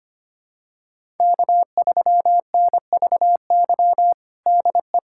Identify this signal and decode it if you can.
Morse code, heard as a keyed tone, about 25 wpm, 700 Hz. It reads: K3NVY DE
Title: USA Callsign DE Callsign - 5 Characters - 25wpm